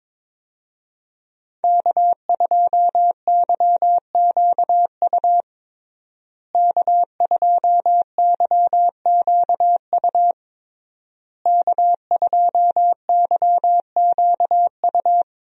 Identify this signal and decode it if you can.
Morse code, heard as a keyed tone, about 22 wpm, 700 Hz. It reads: K2YQU K2YQU K2YQU
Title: USA Call Signs - 5 Characters - Repeated 3x - 22wpm